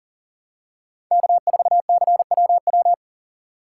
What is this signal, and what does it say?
Morse code, heard as a keyed tone, about 40 wpm, 700 Hz. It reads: K4CWW
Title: USA Call Signs - 5 Characters - 4x Character Spacing - 40wpm